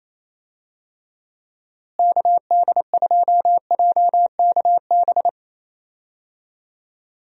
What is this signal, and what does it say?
Morse code, heard as a keyed tone, about 27 wpm, 700 Hz. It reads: KD2JKB
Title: USA Call Signs - 6 Characters - Repeated 2x - 28wpm